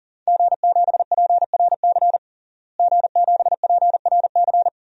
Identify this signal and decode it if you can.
Morse code, heard as a keyed tone, about 40 wpm, 700 Hz. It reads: G7PRC G7PRC
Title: DXCC-100 Call Signs - 5 Characters - Repeated 2x - 40wpm